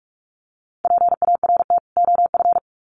Morse code, heard as a keyed tone, about 44 wpm, 700 Hz. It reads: PART OF
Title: Sets of 2 Words 45wpm